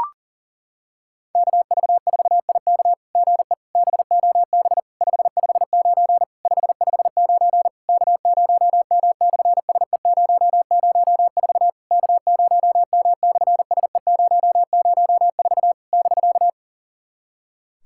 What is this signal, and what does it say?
Morse code, heard as a keyed tone, about 40 wpm, 700 Hz. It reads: KV4IK GE BOB 559 559 K0M/SE004 K0M/SE004 <BK>
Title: SOTA QSO Practice 40wpm